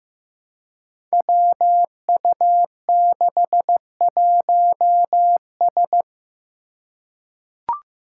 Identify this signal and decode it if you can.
Morse code, heard as a keyed tone, about 15 wpm, 700 Hz. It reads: WU61S
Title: Five Challenging Letter-Number Combinations 15wpm